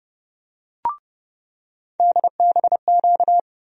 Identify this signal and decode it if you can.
Morse code, heard as a keyed tone, about 30 wpm, 700 Hz. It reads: DBQ
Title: USA Call Sign Postfixes - 3 Letters 30wpm